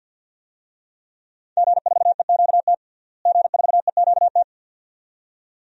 Morse code, heard as a keyed tone, about 47 wpm, 700 Hz. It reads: G4EQT G4EQT
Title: DXCC-100 Call Signs - 5 Characters - Repeated 2x - 50wpm